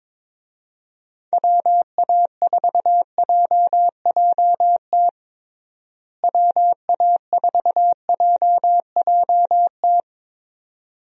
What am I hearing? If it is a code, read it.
Morse code, heard as a keyed tone, about 22 wpm, 700 Hz. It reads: WA4JJT WA4JJT
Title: USA Call Signs - 6 Characters - Repeated 2x - 22wpm